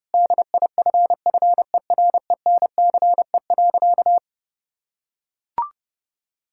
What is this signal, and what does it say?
Morse code, heard as a keyed tone, about 30 wpm, 700 Hz. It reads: DIFFERENCE.
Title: Sentences from Top 1000 Words - Review Part 5 - 30wpm